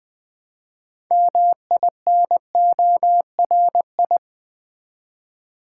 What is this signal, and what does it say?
Morse code, heard as a keyed tone, about 20 wpm, 700 Hz. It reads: MINORI
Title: Build A Word - 2 Letters At A Time - 8-Letter Words - 20wpm